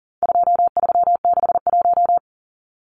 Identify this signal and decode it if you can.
Morse code, heard as a keyed tone, about 40 wpm, 700 Hz. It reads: 2361
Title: Numbers - 4-Digit 40wpm